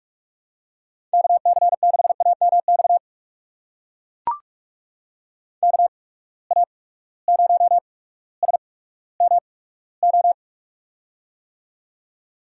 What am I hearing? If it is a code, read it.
Morse code, heard as a keyed tone, about 43 wpm, spaced as for 19 wpm, 700 Hz. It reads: KC6AMX KA0SMO
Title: USA Call Signs - 6 Characters - 8x Character Spacing - 45wpm